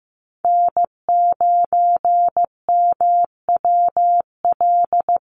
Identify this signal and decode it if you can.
Morse code, heard as a keyed tone, about 15 wpm, 700 Hz. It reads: N9MWL
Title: USA Call Signs Repeated 2x 15wpm